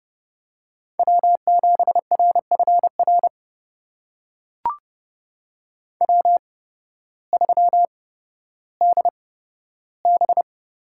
Morse code, heard as a keyed tone, about 30 wpm, spaced as for 14 wpm, 700 Hz. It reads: W7RFR W3DB
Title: USA Call Signs - 8x Character Spacing - 30wpm